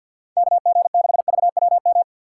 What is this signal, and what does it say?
Morse code, heard as a keyed tone, about 54 wpm, 700 Hz. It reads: K G 6 V W M
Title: USA Call Signs - 6 Characters - 8x Character Spacing - 50wpm